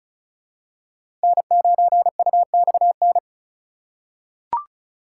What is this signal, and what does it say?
Morse code, heard as a keyed tone, about 36 wpm, 700 Hz. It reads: N9UXN
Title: USA Call Signs - 5 Characters - 6x Character Spacing - 35wpm